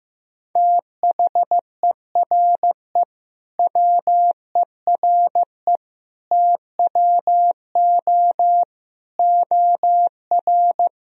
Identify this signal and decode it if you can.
Morse code, heard as a keyed tone, about 15 wpm, 700 Hz. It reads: THERE WERE TWO OR
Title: Sentences from Top 1000 Words - Review Part 1 - 15wpm